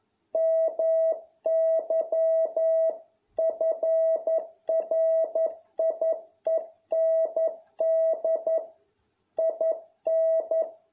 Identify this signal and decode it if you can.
Morse code, heard as a keyed tone, about 11 wpm, 650 Hz. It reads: MY FRIEND IN